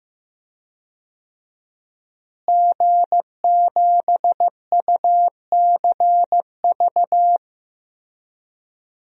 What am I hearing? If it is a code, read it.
Morse code, heard as a keyed tone, about 15 wpm, 700 Hz. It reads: G7UCV
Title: DXCC-100 Call Signs - 5 Characters - Repeated 2x - 15wpm